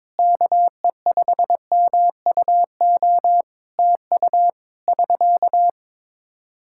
Morse code, heard as a keyed tone, about 22 wpm, 700 Hz. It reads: KE5MUO TU <SK>
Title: K1USN SST QSO Practice 22wpm